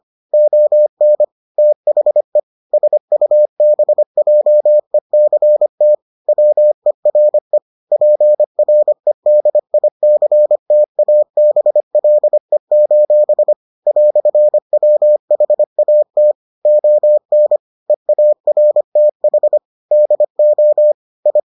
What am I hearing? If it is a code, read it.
Morse code, heard as a keyed tone, about 25 wpm, 600 Hz. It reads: ON THE SUBJECT WERE PREDICTABLE: "WHAT ON EARTH DO I